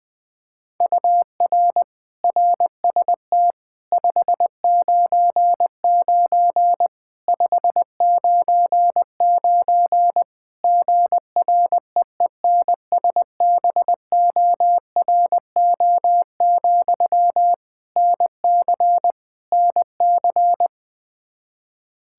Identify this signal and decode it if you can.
Morse code, heard as a keyed tone, about 20 wpm, 700 Hz. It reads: UR RST 599 599 GREENSBORO, NC NC
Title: Practice QSO - Exchange 1 - Beginning - 20wpm